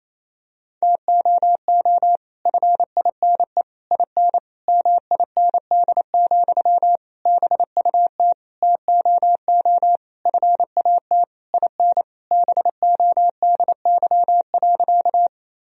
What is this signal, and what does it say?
Morse code, heard as a keyed tone, about 28 wpm, 700 Hz. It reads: TOO FINE IN MIND, BUT TOO FAT IN BODY.
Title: Sentences from Top 1000 Words - Review Part 4 - 28wpm